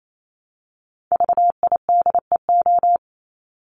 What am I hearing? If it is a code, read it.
Morse code, heard as a keyed tone, about 28 wpm, 700 Hz. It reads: VIDEO
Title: Top 501-750 Words 28wpm